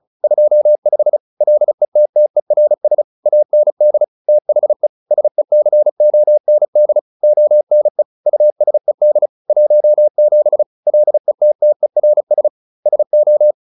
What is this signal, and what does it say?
Morse code, heard as a keyed tone, about 34 wpm, 600 Hz. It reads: 25 LETTERS AND THE SECOND ONE USED 17 LETTERS SO